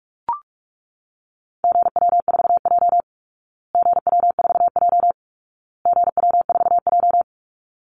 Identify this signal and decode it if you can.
Morse code, heard as a keyed tone, about 44 wpm, 700 Hz. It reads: GW4J GW4J GW4J
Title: Call Signs Repeated 2x - 45wpm